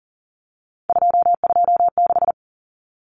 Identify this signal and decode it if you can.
Morse code, heard as a keyed tone, about 40 wpm, 700 Hz. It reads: 226
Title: Common Cut Numbers - 3-Digit 40wpm